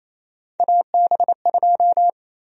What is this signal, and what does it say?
Morse code, heard as a keyed tone, about 28 wpm, 700 Hz. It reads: AB2
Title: USA Call Sign Prefixes - Repeated 2x - 28wpm